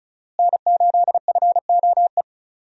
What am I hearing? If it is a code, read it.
Morse code, heard as a keyed tone, about 36 wpm, 700 Hz. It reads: N8FOE
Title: USA Call Signs - 5 Characters - Repeated 2x - 35wpm